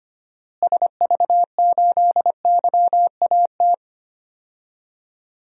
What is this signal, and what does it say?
Morse code, heard as a keyed tone, about 25 wpm, 700 Hz. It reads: SV8YAT
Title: DXCC-100 Call Signs - 6 Characters - Repeated 2x - 25wpm